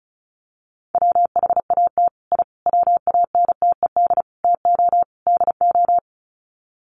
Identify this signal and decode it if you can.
Morse code, heard as a keyed tone, about 34 wpm, 700 Hz. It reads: WHAT I WANTED TO DO
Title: Sets of 5 Words Encore2 35wpm